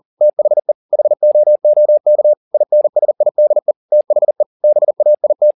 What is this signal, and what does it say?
Morse code, heard as a keyed tone, about 40 wpm, 600 Hz. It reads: THE HOOK INSIDE THE BAIT